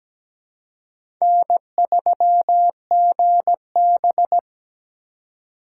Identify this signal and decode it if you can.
Morse code, heard as a keyed tone, about 17 wpm, 700 Hz. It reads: N3GB
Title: USA Call Signs - 4 Characters - 2x Character Spacing - 17wpm